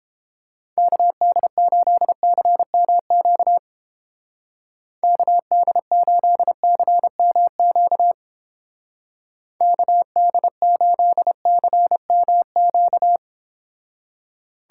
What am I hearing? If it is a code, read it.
Morse code, heard as a keyed tone, about 30 wpm, 700 Hz. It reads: KD8CMQ KD8CMQ KD8CMQ